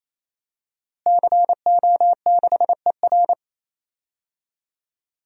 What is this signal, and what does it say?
Morse code, heard as a keyed tone, about 28 wpm, 700 Hz. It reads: CO6ER
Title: DXCC-100 Call Signs - 5 Characters - Repeated 3x - 28wpm